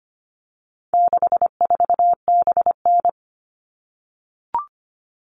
Morse code, heard as a keyed tone, about 25 wpm, 700 Hz. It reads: 64BN
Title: All Cut Numbers - 4-Digit 25wpm